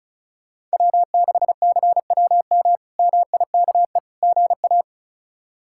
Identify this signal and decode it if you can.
Morse code, heard as a keyed tone, about 34 wpm, 700 Hz. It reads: W6CWM MIKE GA
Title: K1USN SST QSO Practice - 35wpm